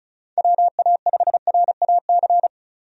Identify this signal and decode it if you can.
Morse code, heard as a keyed tone, about 36 wpm, 700 Hz. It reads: WA5RAC
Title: USA Call Signs - 6 Characters - 35wpm